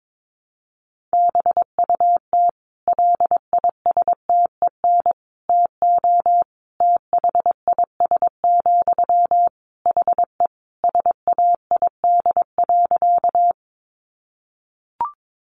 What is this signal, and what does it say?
Morse code, heard as a keyed tone, about 22 wpm, 700 Hz. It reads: BUT LISTEN TO THIS, HE SAID.
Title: Sentences from Top 400 Words 22wpm